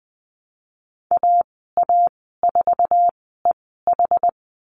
Morse code, heard as a keyed tone, about 20 wpm, spaced as for 15 wpm, 700 Hz. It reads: AA4EH
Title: 088 Lesson 15g - Number 5 - Call Signs 20wpm